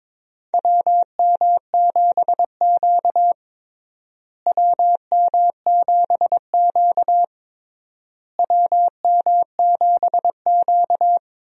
Morse code, heard as a keyed tone, about 22 wpm, 700 Hz. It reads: WM7Q WM7Q WM7Q
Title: USA Call Signs - 4 Characters - Repeated 3x - 22wpm